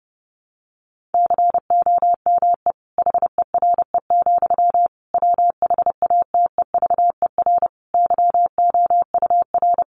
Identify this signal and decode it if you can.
Morse code, heard as a keyed tone, about 30 wpm, 700 Hz. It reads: COME HERE, WHATEVER YOUR
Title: Sentences from Top 1100 Words 30wpm